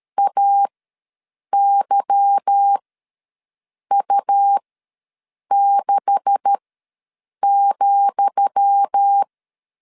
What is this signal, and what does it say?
Morse code, heard as a keyed tone, about 13 wpm, spaced as for 7 wpm, 800 Hz. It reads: AYU6,
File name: LEZIONI / 10 SIGNS / 10D COMA ws 12wpm 3618